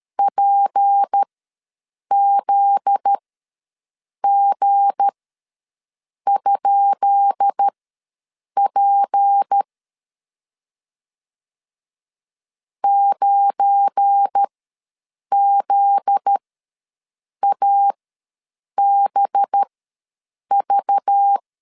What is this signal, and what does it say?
Morse code, heard as a keyed tone, about 13 wpm, spaced as for 7 wpm, 800 Hz. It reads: PZG?P 9ZABV
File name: LEZIONI / 10 SIGNS / 10B INT ws 12wpm 3527